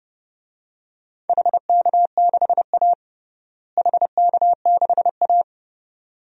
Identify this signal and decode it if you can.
Morse code, heard as a keyed tone, about 30 wpm, 700 Hz. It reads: HK6A HK6A
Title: DXCC-100 Call Signs - 4 Characters - Repeated 2x - 30wpm